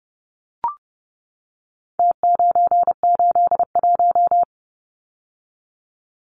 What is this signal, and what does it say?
Morse code, heard as a keyed tone, about 30 wpm, 700 Hz. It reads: T981
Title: Numbers - Contest Style - 4-Digit 30wpm